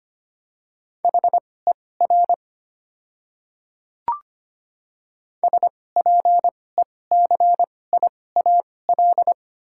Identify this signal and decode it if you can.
Morse code, heard as a keyed tone, about 25 wpm, spaced as for 18 wpm, 700 Hz. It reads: HER SPECIAL
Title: 2x Character Spacing - Top 500 Words - 25wpm